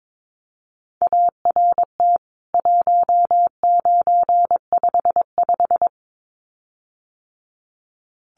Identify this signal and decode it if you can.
Morse code, heard as a keyed tone, about 22 wpm, 700 Hz. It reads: ART 1955
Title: CWT Exchange 22wpm